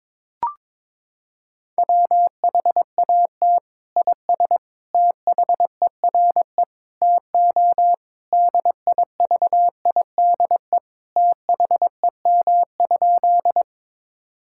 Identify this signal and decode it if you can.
Morse code, heard as a keyed tone, about 22 wpm, 700 Hz. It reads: WHAT IS THERE TO DIVIDE THEM?
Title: Sentences from Top 1600 Words 22wpm